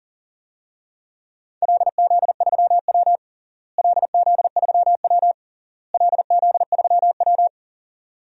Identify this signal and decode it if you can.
Morse code, heard as a keyed tone, about 40 wpm, 700 Hz. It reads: LZ3W LZ3W LZ3W
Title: DXCC-100 Call Signs - 4 Characters - Repeated 3x - 40wpm